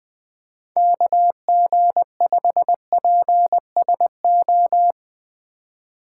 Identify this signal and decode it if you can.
Morse code, heard as a keyed tone, about 21 wpm, 700 Hz. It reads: KG5PSO
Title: USA Call Signs - 6 Characters - 2x Character Spacing - 20wpm